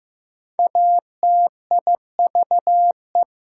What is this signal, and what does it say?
Morse code, heard as a keyed tone, about 15 wpm, 700 Hz. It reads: ATIVE
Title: Prefixes and Suffixes 15wpm